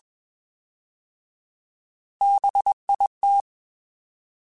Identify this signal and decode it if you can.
Morse code, heard as a keyed tone, about 21 wpm, 800 Hz. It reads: BIT
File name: Alphabetical / Alphabetical mp3s / Bs @20